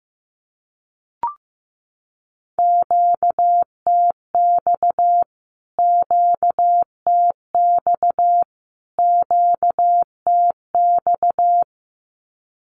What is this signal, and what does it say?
Morse code, heard as a keyed tone, about 15 wpm, 700 Hz. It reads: QTX QTX QTX
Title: Q-Codes Repeated 3x 15wpm